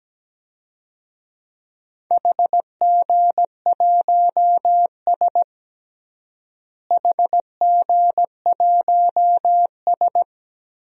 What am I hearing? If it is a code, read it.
Morse code, heard as a keyed tone, about 17 wpm, 700 Hz. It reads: HG1S HG1S
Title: DXCC-100 Call Signs - 4 Characters - Repeated 2x - 17wpm